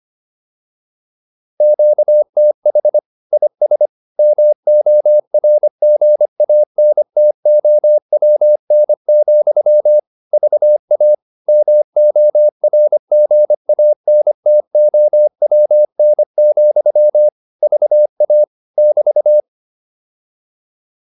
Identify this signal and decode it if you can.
Morse code, heard as a keyed tone, about 25 wpm, 600 Hz. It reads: QTH IS MORGANTOWN, VA MORGANTOWN, VA <BT>